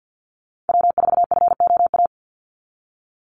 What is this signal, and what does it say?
Morse code, heard as a keyed tone, about 49 wpm, 700 Hz. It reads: W4FOA
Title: Call Signs Repeated 3x - 50wpm